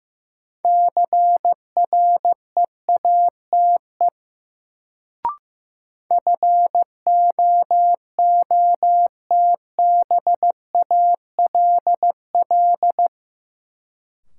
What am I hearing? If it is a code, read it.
Morse code, heard as a keyed tone, about 15 wpm, 700 Hz. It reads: CREATE FOOTBALL
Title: Sentences from Top 1600 Words - New Word Drill Alt - 15wpm